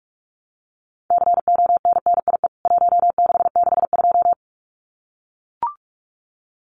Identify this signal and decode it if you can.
Morse code, heard as a keyed tone, about 44 wpm, 700 Hz. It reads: CONNIE 1662
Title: CWT Exchange 45wpm